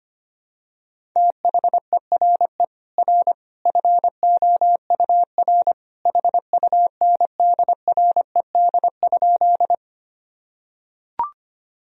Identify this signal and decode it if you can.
Morse code, heard as a keyed tone, about 25 wpm, 700 Hz. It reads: THERE R FOUR HUNDRED?
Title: Sentences from Top 400 Words and Abbreviations 25wpm